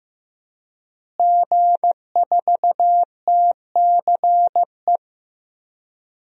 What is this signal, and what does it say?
Morse code, heard as a keyed tone, about 15 wpm, 700 Hz. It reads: G4TCE
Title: DXCC-100 Call Signs - 5 Characters - Repeated 3x - 15wpm